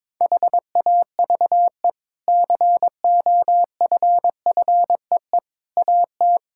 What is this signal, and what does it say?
Morse code, heard as a keyed tone, about 22 wpm, 700 Hz. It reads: HAVE COFFEE AT
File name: practice / 22wpm / Easy Sentences with Hints - 22wpm